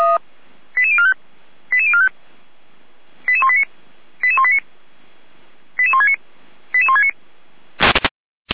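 Five-tone selective call (ZVEI), heard as quick runs of five tones, tones of about 70 ms. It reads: 88045 88045 88189 88189 88169 88169
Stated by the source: ZVEI five-tone selective call received by SDR.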